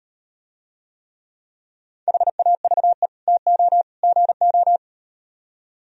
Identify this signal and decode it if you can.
Morse code, heard as a keyed tone, about 38 wpm, 700 Hz. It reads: HAVE TO GO